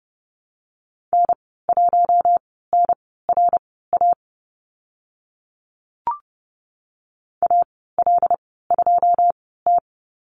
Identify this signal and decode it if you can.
Morse code, heard as a keyed tone, about 30 wpm, spaced as for 17 wpm, 700 Hz. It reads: N1NRA AL2T
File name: Lesson 14g - Number 2 - Call Signs 30wpm